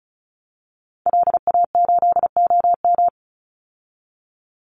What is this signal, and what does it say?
Morse code, heard as a keyed tone, about 34 wpm, 700 Hz. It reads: LA8OM
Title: Call Signs Repeated 3x - 35wpm